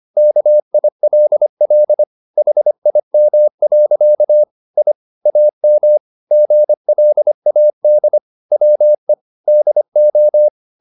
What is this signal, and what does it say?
Morse code, heard as a keyed tone, about 25 wpm, 600 Hz. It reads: KILL HIM. I AM GLAD WE DO